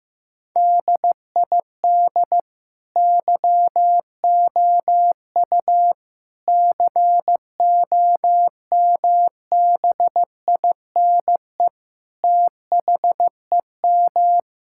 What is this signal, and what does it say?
Morse code, heard as a keyed tone, about 16 wpm, 700 Hz. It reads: DID YOU COMBINE THEM
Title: Sentences from Top 1600 Words 15wpm